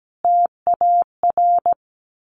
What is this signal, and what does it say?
Morse code, heard as a keyed tone, about 17 wpm, 700 Hz. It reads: TAR
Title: A Sound Of Thunder 17wpm